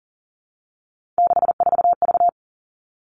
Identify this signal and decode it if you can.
Morse code, heard as a keyed tone, about 40 wpm, 700 Hz. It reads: 64V
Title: All Cut Numbers - 3-Digit 40wpm